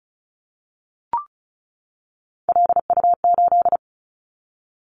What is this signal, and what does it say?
Morse code, heard as a keyed tone, about 36 wpm, 690 Hz. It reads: LU8
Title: Callsign Prefixes for Top 50 DXCC - 35wpm